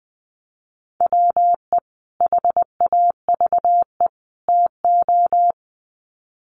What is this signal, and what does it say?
Morse code, heard as a keyed tone, about 20 wpm, 700 Hz. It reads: WE HAVE TO
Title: Sets of 3 Words 20wpm